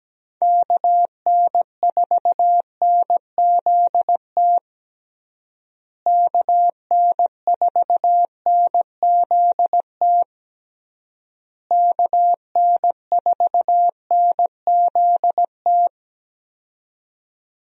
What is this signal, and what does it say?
Morse code, heard as a keyed tone, about 17 wpm, 700 Hz. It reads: KN4NZT KN4NZT KN4NZT
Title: USA Call Signs - 6 Characters - Repeated 3x - 17wpm